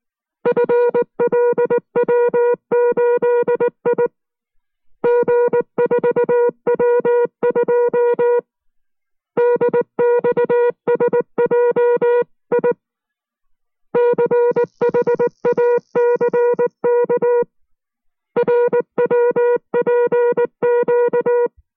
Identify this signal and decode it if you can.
Morse code, heard as a keyed tone, about 19 wpm, 480 Hz. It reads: FLW8I G4W2 DXSJI CHACK RWPQ